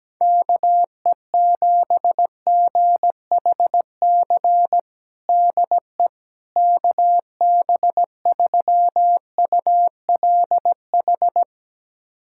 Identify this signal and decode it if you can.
Morse code, heard as a keyed tone, about 17 wpm, 700 Hz. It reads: KE7GHC DE KB3ULH
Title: USA Callsign DE Callsign - 6 Characters - 17wpm